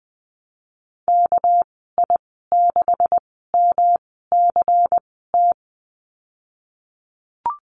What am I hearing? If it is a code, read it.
Morse code, heard as a keyed tone, about 19 wpm, spaced as for 14 wpm, 700 Hz. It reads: KI6MCT
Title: 200 Lesson 32g - Letter k - Call Signs 20wpm